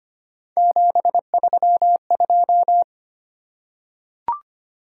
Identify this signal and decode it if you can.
Morse code, heard as a keyed tone, about 25 wpm, 700 Hz. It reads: 732
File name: Three Letter-Number Combinations - 25wpm